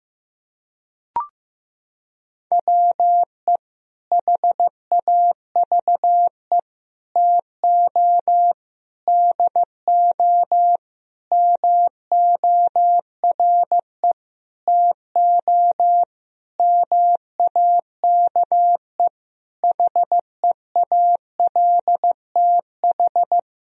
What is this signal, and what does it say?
Morse code, heard as a keyed tone, about 15 wpm, 700 Hz. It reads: WE HAVE TO DO MORE TO MAKE HEALTH
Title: Sentences from Top 1000 Words - Review Part 3 - 15wpm